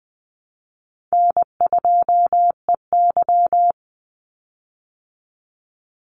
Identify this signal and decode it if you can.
Morse code, heard as a keyed tone, about 20 wpm, 700 Hz. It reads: N2EY
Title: Call Signs Repeated 2x - 20wpm